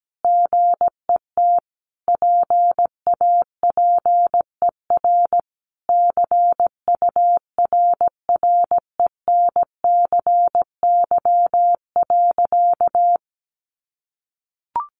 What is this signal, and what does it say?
Morse code, heard as a keyed tone, about 17 wpm, 700 Hz. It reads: GET PAPER CURRENCY.
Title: Sentences from Top 1500 Words 17wpm